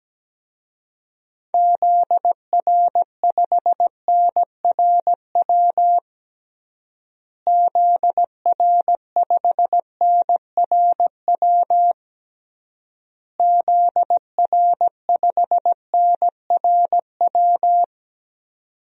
Morse code, heard as a keyed tone, about 17 wpm, 700 Hz. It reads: ZR5NRW ZR5NRW ZR5NRW
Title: DXCC-100 Call Signs - 6 Characters - Repeated 3x - 17wpm